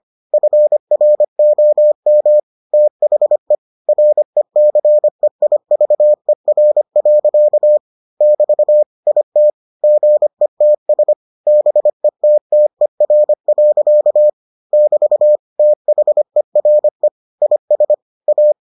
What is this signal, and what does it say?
Morse code, heard as a keyed tone, about 25 wpm, 600 Hz. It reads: FROM THE RECEIVER. <BT> IT GETS BETTER. <BT> THERE IS A